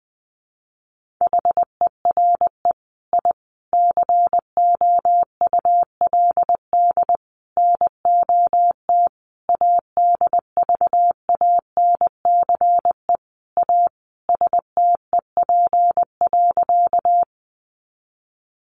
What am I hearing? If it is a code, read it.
Morse code, heard as a keyed tone, about 20 wpm, 700 Hz. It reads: HERE I COULD NOT ADVANCE A STEP.
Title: Sentences from Top 1400 Words 20wpm